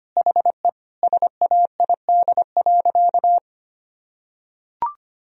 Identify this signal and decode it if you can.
Morse code, heard as a keyed tone, about 25 wpm, 700 Hz. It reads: HE SAID.
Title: Sentences from Top 1000 Words - Review Part 4 - 25wpm